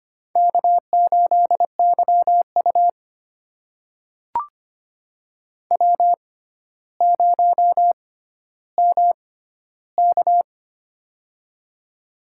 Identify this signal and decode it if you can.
Morse code, heard as a keyed tone, about 25 wpm, spaced as for 13 wpm, 700 Hz. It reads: K8YU W0MK
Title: USA Call Signs - 4 Characters - 6x Character Spacing - 25wpm